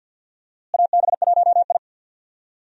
Morse code, heard as a keyed tone, about 52 wpm, 700 Hz. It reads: AB1I
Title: USA Call Signs - 4 Characters - Repeated 3x - 50wpm